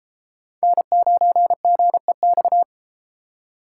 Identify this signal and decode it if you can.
Morse code, heard as a keyed tone, about 30 wpm, 700 Hz. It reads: N9GEX